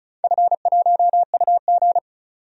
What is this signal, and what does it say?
Morse code, heard as a keyed tone, about 34 wpm, 700 Hz. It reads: F1UG
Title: DXCC-100 Call Signs - 4 Characters - 35wpm